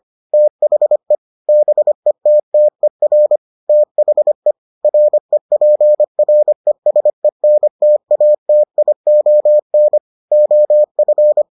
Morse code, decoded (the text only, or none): THE BETTER THE REPRESENTATION OF